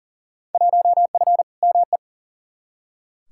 Morse code, heard as a keyed tone, about 40 wpm, 700 Hz. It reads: 1F ME
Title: Field Day Exchanges Encore - 40wpm